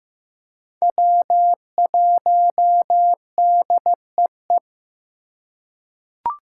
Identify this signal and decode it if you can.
Morse code, heard as a keyed tone, about 15 wpm, 700 Hz. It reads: W1DEE
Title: USA Call Signs - 2x Character Spacing - 15wpm